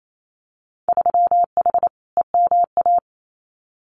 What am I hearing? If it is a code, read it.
Morse code, heard as a keyed tone, about 28 wpm, 700 Hz. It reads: 3H EMA
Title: Winter Field Day Exchanges Encore 28wpm